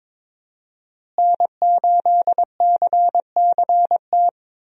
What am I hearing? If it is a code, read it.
Morse code, heard as a keyed tone, about 22 wpm, 700 Hz. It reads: N8CCT
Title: USA Call Signs - 5 Characters - Repeated 3x - 22wpm